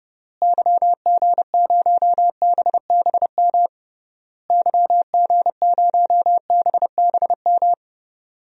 Morse code, heard as a keyed tone, about 30 wpm, 700 Hz. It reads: YG0BBM YG0BBM
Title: DXCC-100 Call Signs - 6 Characters - Repeated 3x - 30wpm